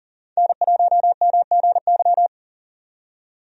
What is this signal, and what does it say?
Morse code, heard as a keyed tone, about 39 wpm, 700 Hz. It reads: N1MGY
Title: USA Call Signs - 5 Characters - 40wpm